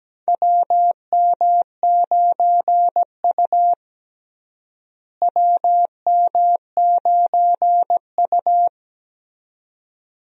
Morse code, heard as a keyed tone, about 17 wpm, 700 Hz. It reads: WM9U WM9U
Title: USA Call Signs - 4 Characters - Repeated 2x - 17wpm